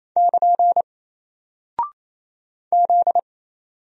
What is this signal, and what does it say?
Morse code, heard as a keyed tone, about 28 wpm, 700 Hz. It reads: <KN> Z
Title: Single Character (Letter:Number:Punctuation:Prosign) Alt 28wpm